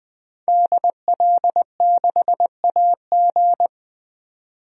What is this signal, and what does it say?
Morse code, heard as a keyed tone, about 20 wpm, 700 Hz. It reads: DL6AG
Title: DXCC-100 Call Signs - 5 Characters - 20wpm